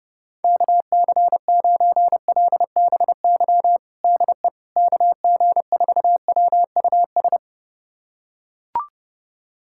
Morse code, heard as a keyed tone, about 30 wpm, 700 Hz. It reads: KC9LBY DE KG4WUS
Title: USA Callsign DE Callsign - 6 Characters - 30wpm